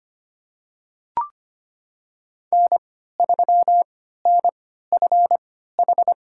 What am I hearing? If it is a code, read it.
Morse code, heard as a keyed tone, about 25 wpm, spaced as for 14 wpm, 700 Hz. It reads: N3NFH
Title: USA Call Signs - 5 Characters - 3x Character Spacing - 25wpm